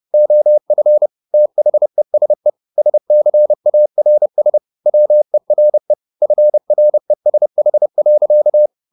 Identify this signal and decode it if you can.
Morse code, heard as a keyed tone, about 30 wpm, 600 Hz. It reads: OF THESE SCARS WERE FRESH.
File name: the old man and the sea / the old man and the sea 1 30wpm